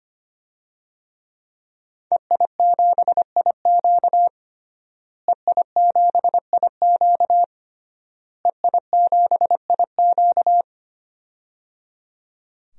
Morse code, heard as a keyed tone, about 25 wpm, 700 Hz. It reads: EI7IQ EI7IQ EI7IQ
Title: DXCC-100 Call Signs - 5 Characters - Repeated 3x - 25wpm